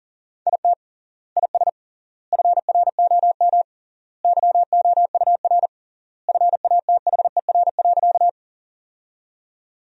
Morse code, heard as a keyed tone, about 40 wpm, 700 Hz. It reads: IT IS FROM YOUR FATHER.
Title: 3x Word-Spacing - Sentences from Top 300 Words - 40wpm